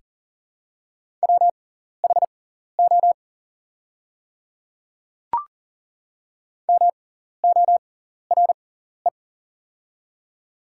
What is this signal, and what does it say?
Morse code, heard as a keyed tone, about 40 wpm, spaced as for 14 wpm, 700 Hz. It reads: WHO MORE
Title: 6x Character Spacing - Top 100 Words - 40wpm